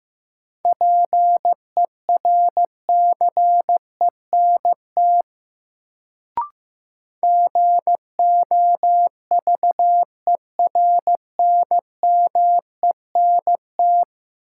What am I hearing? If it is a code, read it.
Morse code, heard as a keyed tone, about 15 wpm, 700 Hz. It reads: PERCENT GOVERNMENT
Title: Sets of 2 Words - New Word Drill - 15wpm